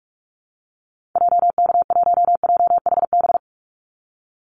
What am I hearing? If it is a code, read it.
Morse code, heard as a keyed tone, about 44 wpm, 700 Hz. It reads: JK1JHB
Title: Call Signs Repeated 2x - 45wpm